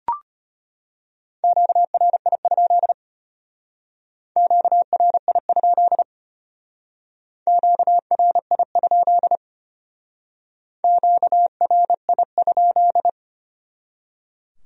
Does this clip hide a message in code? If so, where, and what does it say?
Yes: Morse code from 1.4 to 13.1 s, heard as a keyed tone, about 32 wpm, 700 Hz. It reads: QRI? QRI? QRI? QRI?